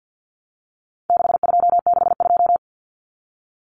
Morse code, heard as a keyed tone, about 50 wpm, 700 Hz. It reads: 6262
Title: Numbers - 4-Digit 50wpm